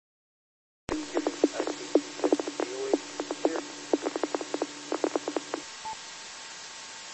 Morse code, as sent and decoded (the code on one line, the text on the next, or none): none